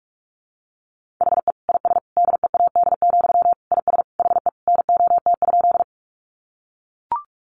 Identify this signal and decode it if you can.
Morse code, heard as a keyed tone, about 44 wpm, 700 Hz. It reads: HE IS DEAD, IS HE NOT?
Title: Sentences from Top 1500 Words 45wpm